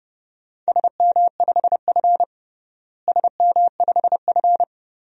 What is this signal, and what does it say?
Morse code, heard as a keyed tone, about 32 wpm, 700 Hz. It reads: SM5F SM5F
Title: DXCC-100 Call Signs - 4 Characters - Repeated 3x - 30wpm